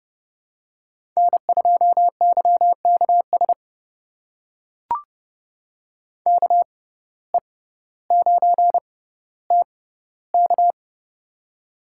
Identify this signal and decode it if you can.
Morse code, heard as a keyed tone, about 30 wpm, spaced as for 15 wpm, 700 Hz. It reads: N2YKS KE9TK
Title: USA Call Signs - 5 Characters - 6x Character Spacing - 30wpm